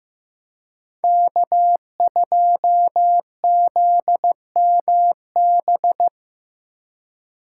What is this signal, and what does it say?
Morse code, heard as a keyed tone, about 18 wpm, 700 Hz. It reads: K2ZMB
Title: Speed Racing - USA 1x3 Call Signs 15wpm